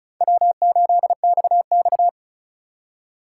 Morse code, heard as a keyed tone, about 36 wpm, 700 Hz. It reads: W8XX
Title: Speed Racing - USA Call Signs 35wpm